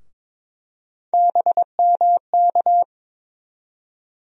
Morse code, heard as a keyed tone, about 22 wpm, 700 Hz. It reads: BMK
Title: USA Call Sign Postfixes - 3 Letters 22wpm